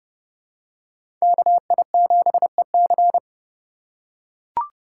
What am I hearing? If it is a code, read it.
Morse code, heard as a keyed tone, about 30 wpm, 700 Hz. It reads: KI7EC
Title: USA Call Signs - 5 Characters - 3x Character Spacing - 30wpm